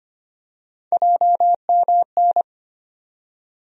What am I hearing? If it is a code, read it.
Morse code, heard as a keyed tone, about 25 wpm, 700 Hz. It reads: JMN
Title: USA Call Sign Postfixes - 3 Letters 25wpm